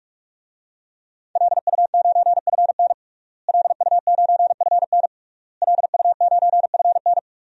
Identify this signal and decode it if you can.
Morse code, heard as a keyed tone, about 46 wpm, 700 Hz. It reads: LU9FN LU9FN LU9FN
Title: DXCC-100 Call Signs - 5 Characters - Repeated 3x - 45wpm